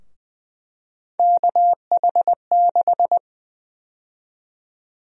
Morse code, heard as a keyed tone, about 20 wpm, 700 Hz. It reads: KH6
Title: USA Call Sign Prefixes - Repeated 2x - 20wpm